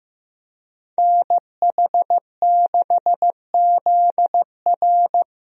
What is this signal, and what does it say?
Morse code, heard as a keyed tone, about 15 wpm, 700 Hz. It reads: NH6ZR
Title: USA Call Signs - 5 Characters - 15wpm